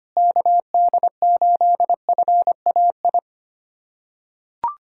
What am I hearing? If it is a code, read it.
Morse code, heard as a keyed tone, about 25 wpm, 700 Hz. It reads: KD8FAI
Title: USA Callsign DE Callsign 25wpm